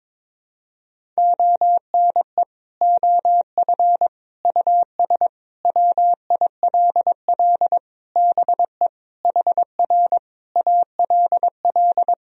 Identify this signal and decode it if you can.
Morse code, heard as a keyed tone, about 22 wpm, 700 Hz. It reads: ONE OF US WILL BE HR ALL
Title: Sentences from Top 300 Words and Abbreviations 22wpm